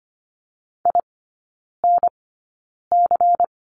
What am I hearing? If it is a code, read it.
Morse code, heard as a keyed tone, about 25 wpm, spaced as for 9 wpm, 700 Hz. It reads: INC
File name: course / 25:10wpm / 093 Lesson 16e - Letter c - Three Characters 25wpm